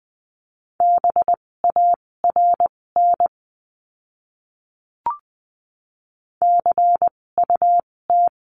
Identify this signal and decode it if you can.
Morse code, heard as a keyed tone, about 20 wpm, spaced as for 16 wpm, 700 Hz. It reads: BARN CUT
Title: 256 Lesson 42a - Words 12-20wpm